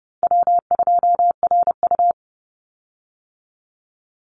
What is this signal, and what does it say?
Morse code, heard as a keyed tone, about 30 wpm, 700 Hz. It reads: W2RU
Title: Call Signs Repeated 3x - 30wpm